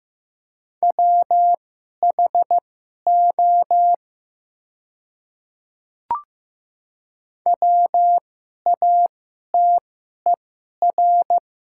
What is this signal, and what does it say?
Morse code, heard as a keyed tone, about 15 wpm, spaced as for 11 wpm, 700 Hz. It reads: WHO WATER
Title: 2x Character Spacing - Top 100 Words - 15wpm